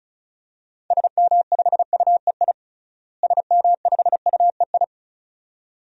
Morse code, heard as a keyed tone, about 36 wpm, 700 Hz. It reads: SM5UEI SM5UEI
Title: DXCC-100 Call Signs - 6 Characters - Repeated 2x - 35wpm